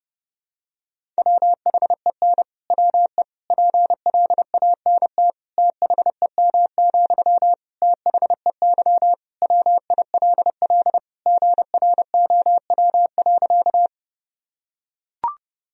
Morse code, heard as a keyed tone, about 32 wpm, 700 Hz. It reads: WHEN WE PLANT THEM, THEY WILL GROW.
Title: Sentences from Top 1000 Words - Review Part 1 - 30wpm